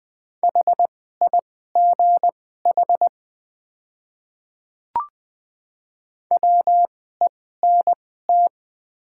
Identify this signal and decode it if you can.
Morse code, heard as a keyed tone, about 20 wpm, spaced as for 14 wpm, 700 Hz. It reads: HIGH WENT
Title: 2x Character Spacing - Top 200 Words - 20wpm